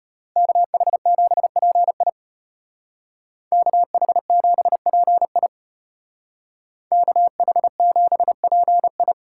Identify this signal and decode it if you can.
Morse code, heard as a keyed tone, about 30 wpm, 700 Hz. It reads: KH7PI KH7PI KH7PI
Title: Speed Racing - USA 2x2 Call Signs 25wpm